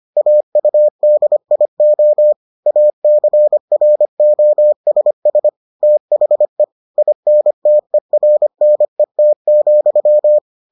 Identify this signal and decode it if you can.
Morse code, heard as a keyed tone, about 25 wpm, 600 Hz. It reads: AUDIO ACROSS THE INTERNET,